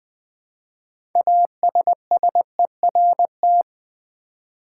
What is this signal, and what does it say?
Morse code, heard as a keyed tone, about 20 wpm, 700 Hz. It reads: ASSERT
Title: 6 Letter Words - First 3 Letters Given - 20wpm